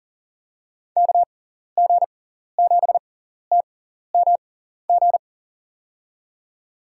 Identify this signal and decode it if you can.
Morse code, heard as a keyed tone, about 37 wpm, spaced as for 14 wpm, 700 Hz. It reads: KG7TMG
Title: USA Call Signs - 6x Character Spacing - 40wpm